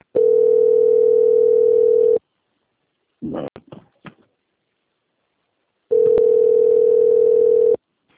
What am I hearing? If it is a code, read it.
Ringback tone (North America), 440 + 480 Hz.